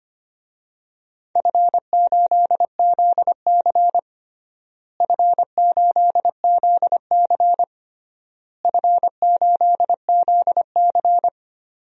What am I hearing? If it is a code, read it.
Morse code, heard as a keyed tone, about 25 wpm, 700 Hz. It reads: F8ZC F8ZC F8ZC
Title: DXCC-100 Call Signs - 4 Characters - Repeated 3x - 25wpm